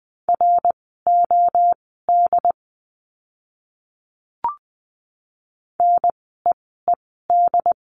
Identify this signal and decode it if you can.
Morse code, heard as a keyed tone, about 20 wpm, spaced as for 14 wpm, 700 Hz. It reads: ROD NEED
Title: 068 Lesson 12f - Letter d - Words 20wpm